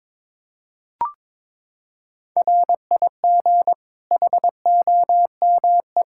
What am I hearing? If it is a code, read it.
Morse code, heard as a keyed tone, about 21 wpm, 700 Hz. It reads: RIG HOME
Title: Practice QSO - Exchange 2 - Info - 22wpm